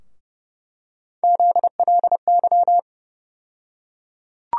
Morse code, heard as a keyed tone, about 30 wpm, 700 Hz. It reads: ZLY
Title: USA Call Sign Postfixes - 3 Letters 30wpm